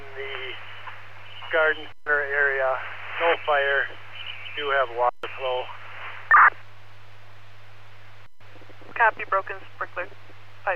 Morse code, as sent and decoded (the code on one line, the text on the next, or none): none